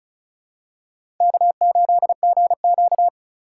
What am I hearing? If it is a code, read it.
Morse code, heard as a keyed tone, about 36 wpm, 700 Hz. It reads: K8GQ
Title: USA Call Signs - 4 Characters - 2x Character Spacing - 35wpm